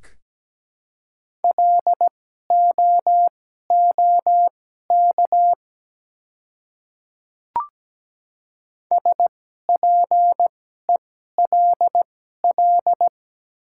Morse code, heard as a keyed tone, about 17 wpm, spaced as for 12 wpm, 700 Hz. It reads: LOOK SPELL
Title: 2x Character Spacing - Top 200 Words - 17wpm